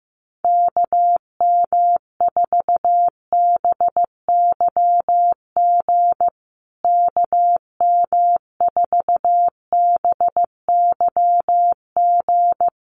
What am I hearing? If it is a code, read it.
Morse code, heard as a keyed tone, about 15 wpm, 700 Hz. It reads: KM4BYG KM4BYG
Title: USA Call Signs Repeated 3x 15wpm